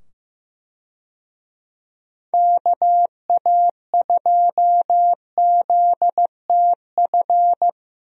Morse code, heard as a keyed tone, about 15 wpm, 700 Hz. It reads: KA2ZTF
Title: USA Call Signs - 6 Characters - 15wpm